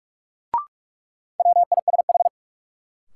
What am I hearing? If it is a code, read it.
Morse code, heard as a keyed tone, about 47 wpm, 700 Hz. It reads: WISH
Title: Sentences from Top 1600 Words - New Word Drill Alt - 45wpm